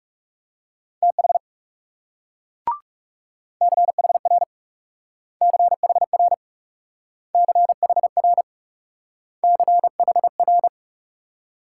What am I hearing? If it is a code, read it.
Morse code, heard as a keyed tone, about 39 wpm, 700 Hz. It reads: TH CHR CHR CHR CHR